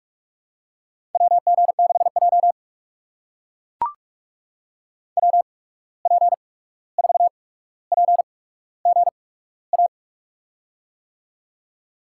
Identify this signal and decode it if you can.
Morse code, heard as a keyed tone, about 44 wpm, spaced as for 17 wpm, 700 Hz. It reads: WG6J WP4PGA
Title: USA Call Signs - 8x Character Spacing - 45wpm